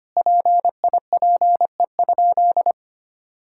Morse code, heard as a keyed tone, about 25 wpm, 700 Hz. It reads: PIPE?
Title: Sentences from Top 1600 Words 25wpm